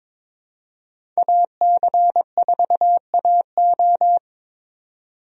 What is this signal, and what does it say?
Morse code, heard as a keyed tone, about 22 wpm, 700 Hz. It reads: AC4AO
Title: USA Call Signs - 5 Characters - 22wpm